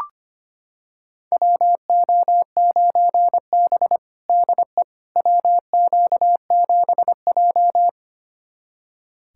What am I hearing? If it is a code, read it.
Morse code, heard as a keyed tone, about 25 wpm, 700 Hz. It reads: WO9B DE WQ7J
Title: USA Callsign DE Callsign - 4 Characters - 25wpm